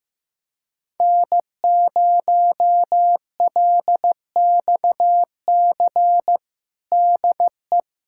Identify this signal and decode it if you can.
Morse code, heard as a keyed tone, about 15 wpm, 700 Hz. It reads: N0LXC DE
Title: USA Callsign DE Callsign - 5 Characters - 15wpm